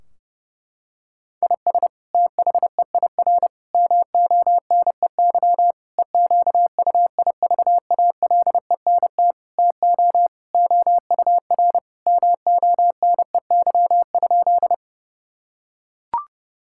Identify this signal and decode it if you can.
Morse code, heard as a keyed tone, about 30 wpm, 700 Hz. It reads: IS THEIR MONEY EQUIVALENT TO OUR MONEY?
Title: Sentences from Top 1600 Words 30wpm